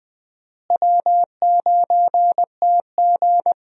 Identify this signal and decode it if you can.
Morse code, heard as a keyed tone, about 20 wpm, 700 Hz. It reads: W9TG
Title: USA Call Signs - 4 Characters - Repeated 2x - 20wpm